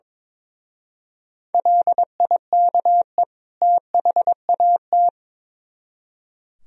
Morse code, heard as a keyed tone, about 24 wpm, 700 Hz. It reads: LIKE THAT
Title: Speed-Racing - Sets of 2 Words 22wpm